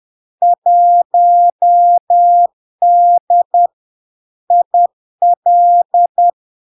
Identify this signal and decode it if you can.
Morse code, heard as a keyed tone, about 10 wpm, 700 Hz. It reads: 1D IL
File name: LARK-FD-2022-Log-in-CW-10wpm-700-Hz